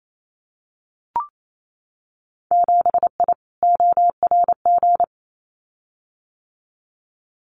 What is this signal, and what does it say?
Morse code, heard as a keyed tone, about 28 wpm, 700 Hz. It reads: 7I ORG
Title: Winter Field Day Exchanges Encore 28wpm